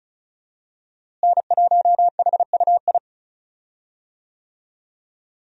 Morse code, heard as a keyed tone, about 36 wpm, 700 Hz. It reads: N1HUI
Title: USA Call Signs - 5 Characters - Repeated 3x - 35wpm